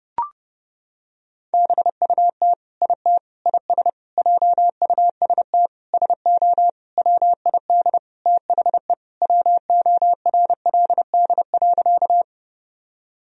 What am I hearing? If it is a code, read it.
Morse code, heard as a keyed tone, about 30 wpm, 700 Hz. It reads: BUT IT IS JUST SO WID THE WORLD.
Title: Sentences from Top 200 Words and Abbreviations 30wpm